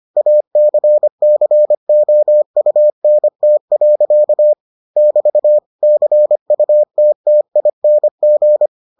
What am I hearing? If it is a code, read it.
Morse code, heard as a keyed tone, about 25 wpm, 600 Hz. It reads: ACCOUNT. <BT> CUTTING